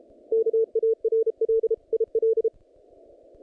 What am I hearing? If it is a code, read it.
Morse code, heard as a keyed tone, about 32 wpm, 450 Hz. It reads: KARL IL
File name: contests / naqp-cw-k5tr-op-k5pi- -2200